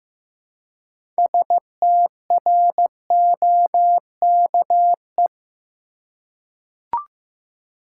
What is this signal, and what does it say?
Morse code, heard as a keyed tone, about 15 wpm, 700 Hz. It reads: STROKE
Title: Six Letter Words - 15wpm